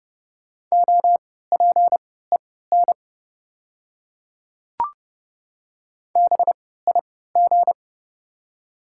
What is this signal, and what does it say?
Morse code, heard as a keyed tone, about 30 wpm, spaced as for 17 wpm, 700 Hz. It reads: OPEN BIG
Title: 3x Character Spacing - Top 300 Words - 30wpm